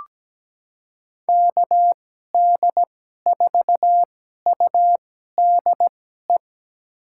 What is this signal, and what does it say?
Morse code, heard as a keyed tone, about 17 wpm, spaced as for 12 wpm, 700 Hz. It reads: KD4UDE
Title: USA Call Signs - 6 Characters - 2x Character Spacing - 17wpm